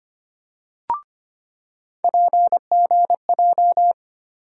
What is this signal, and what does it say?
Morse code, heard as a keyed tone, about 25 wpm, 700 Hz. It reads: PGJ
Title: USA Call Sign Postfixes - 3 Letters 25wpm